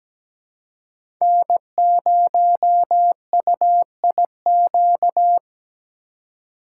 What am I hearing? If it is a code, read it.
Morse code, heard as a keyed tone, about 17 wpm, 700 Hz. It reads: N0UIQ
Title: USA Call Signs - 5 Characters - 2x Character Spacing - 17wpm